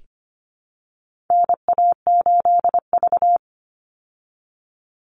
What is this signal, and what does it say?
Morse code, heard as a keyed tone, about 25 wpm, 700 Hz. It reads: NA8V
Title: Call Signs Repeated 3x - 25wpm